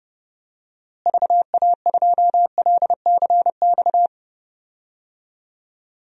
Morse code, heard as a keyed tone, about 32 wpm, 700 Hz. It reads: VA2LCX
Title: DXCC-100 Call Signs - 6 Characters - Repeated 2x - 30wpm